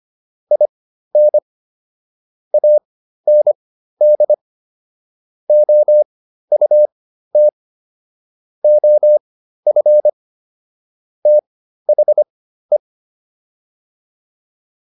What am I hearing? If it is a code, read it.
Morse code, heard as a keyed tone, about 25 wpm, spaced as for 13 wpm, 600 Hz. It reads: IN AND OUT OF THE